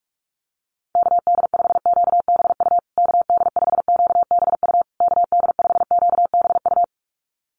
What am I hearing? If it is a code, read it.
Morse code, heard as a keyed tone, about 44 wpm, 700 Hz. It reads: KD5QBU KD5QBU KD5QBU
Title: USA Call Signs Repeated 3x 45wpm